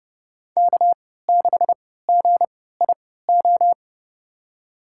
Morse code, heard as a keyed tone, about 30 wpm, spaced as for 17 wpm, 700 Hz. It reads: K6GIO
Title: USA Call Signs - 5 Characters - 3x Character Spacing - 30wpm